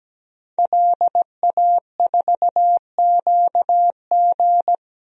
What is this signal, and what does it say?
Morse code, heard as a keyed tone, about 17 wpm, 700 Hz. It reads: LA4QG
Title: DXCC-100 Call Signs - 5 Characters - Repeated 2x - 17wpm